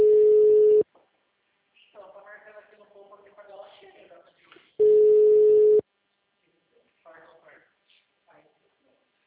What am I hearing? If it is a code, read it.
Ringback tone (Europe, 425 Hz), with speech.
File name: out- -2027- - - .8357